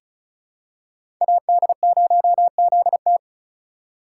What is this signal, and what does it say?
Morse code, heard as a keyed tone, about 36 wpm, 700 Hz. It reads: AD0ZT
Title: USA Call Signs - 5 Characters - 4x Character Spacing - 35wpm